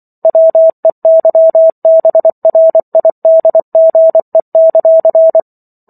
Morse code, heard as a keyed tone, about 24 wpm, 650 Hz. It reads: WEYBRIDGE;